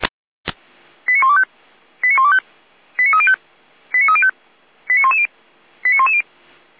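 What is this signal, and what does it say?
Five-tone selective call (ZVEI), heard as quick runs of five tones, tones of about 80 ms. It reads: 89125 89125 89305 89305 89110 89110
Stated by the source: ZVEI five-tone selective call received on SDR for a fire brigade.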